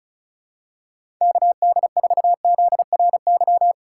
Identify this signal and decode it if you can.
Morse code, heard as a keyed tone, about 36 wpm, 700 Hz. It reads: KD4ZRY
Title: USA Call Signs - 4x Character Spacing - 35wpm